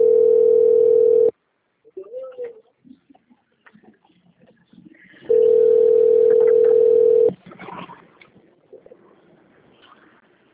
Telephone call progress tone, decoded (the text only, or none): ringback tone